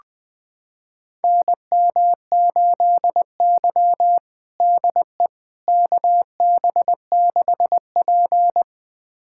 Morse code, heard as a keyed tone, about 21 wpm, 700 Hz. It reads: NM8Y DE KB6P
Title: USA Callsign DE Callsign - 4 Characters - 20wpm